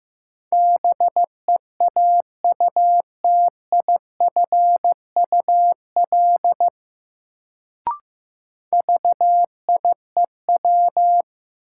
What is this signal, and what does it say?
Morse code, heard as a keyed tone, about 15 wpm, 700 Hz. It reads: BEAUTIFUL VIEW
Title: Sentences from Top 900 Words - New Word Drill - 15wpm